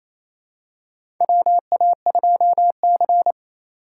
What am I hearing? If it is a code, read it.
Morse code, heard as a keyed tone, about 28 wpm, 700 Hz. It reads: WA2C
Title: USA Call Signs - 4 Characters - 3x Character Spacing - 28wpm